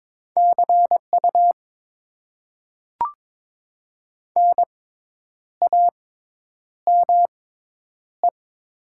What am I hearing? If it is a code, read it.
Morse code, heard as a keyed tone, about 22 wpm, spaced as for 9 wpm, 700 Hz. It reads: CU NAME
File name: Top 90 QSO Elements - 6x Character Spacing - 22wpm